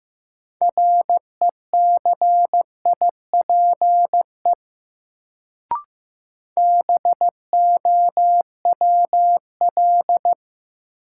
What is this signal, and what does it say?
Morse code, heard as a keyed tone, about 15 wpm, 700 Hz. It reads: RECIPE BOWL
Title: Sentences from Top 1200 Words - New Word Drill - 15wpm